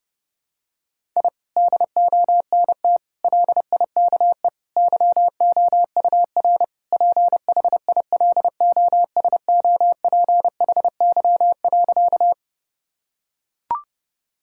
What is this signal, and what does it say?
Morse code, heard as a keyed tone, about 32 wpm, 700 Hz. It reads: I DONT LIKE YOUR PHILOSOPHY.
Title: Sentences from Top 1000 Words - Review Part 6 - 30wpm